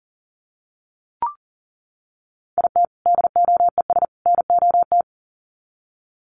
Morse code, heard as a keyed tone, about 40 wpm, 700 Hz. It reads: IT DOES NOT
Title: Sets of 3 Words 40wpm